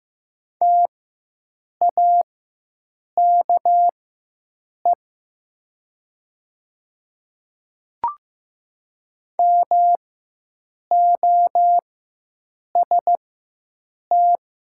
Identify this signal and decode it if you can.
Morse code, heard as a keyed tone, about 15 wpm, spaced as for 7 wpm, 700 Hz. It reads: TAKE MOST
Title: 4x Character Spacing - Top 200 Words - 15wpm